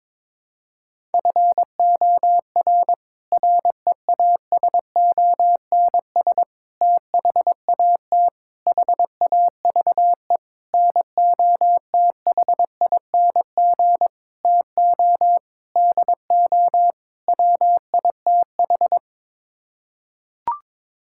Morse code, heard as a keyed tone, about 22 wpm, 700 Hz. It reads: FOR REASONS THAT HAVE NOTHING TO DO WITH
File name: practice / 22wpm / Sets of 8 Words - 22wpm